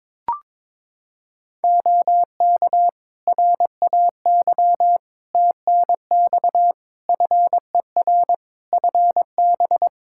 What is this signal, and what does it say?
Morse code, heard as a keyed tone, about 22 wpm, 700 Hz. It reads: OK RAY TNX FER FB
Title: Practice QSO - Exchange 4 - Ending - 22wpm